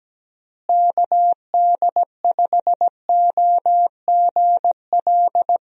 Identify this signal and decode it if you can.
Morse code, heard as a keyed tone, about 17 wpm, 700 Hz. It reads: KD5OGL
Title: USA Call Signs - 6 Characters - Repeated 3x - 17wpm